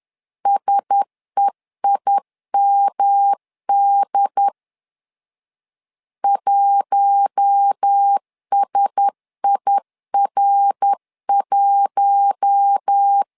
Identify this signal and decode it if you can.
Morse code, heard as a keyed tone, about 10 wpm, 800 Hz. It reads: SEIMD 1SIR1
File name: LEZIONI / 4 Q1S 10wpm 2133